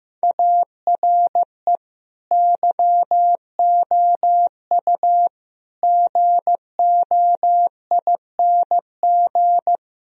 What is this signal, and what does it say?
Morse code, heard as a keyed tone, about 15 wpm, 700 Hz. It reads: ARE YOU GOING
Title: Sentences from Top 1000 Words - Review Part 2 - 15wpm